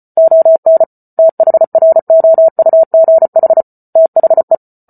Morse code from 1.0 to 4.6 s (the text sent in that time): THROUGH THE